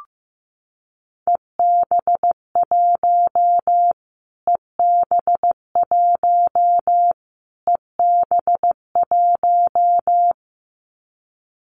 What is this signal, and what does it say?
Morse code, heard as a keyed tone, about 15 wpm, 700 Hz. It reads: EB1 EB1 EB1
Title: Callsign Prefixes for Top 25 DXCC - Repeated 3x - 15wpm